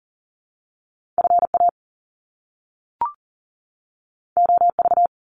Morse code, heard as a keyed tone, about 40 wpm, 700 Hz. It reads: FA OV
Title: Top Two Letter Combinations in Words 40wpm